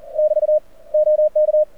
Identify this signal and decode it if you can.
Morse code, heard as a keyed tone, about 42 wpm, 590 Hz. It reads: <BT> OK
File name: archived / CRSnet